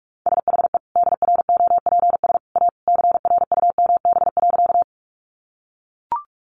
Morse code, heard as a keyed tone, about 44 wpm, 700 Hz. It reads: SHE DROPS A CRUMB.
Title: Additional Easy Phrases 45wpm